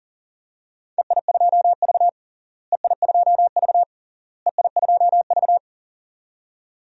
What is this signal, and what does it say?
Morse code, heard as a keyed tone, about 40 wpm, 700 Hz. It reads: EI2V EI2V EI2V
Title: DXCC-100 Call Signs - 4 Characters - Repeated 3x - 40wpm